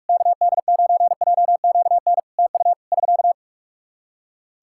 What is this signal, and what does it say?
Morse code, heard as a keyed tone, about 41 wpm, 700 Hz. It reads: KD9JQN TU <SK>
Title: Practice QSO - Exchange 4 - Ending - 45wpm